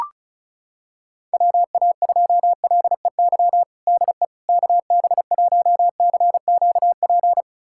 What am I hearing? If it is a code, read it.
Morse code, heard as a keyed tone, about 34 wpm, 700 Hz. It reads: WA2LEY DE KB1CQP
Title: USA Callsign DE Callsign 35wpm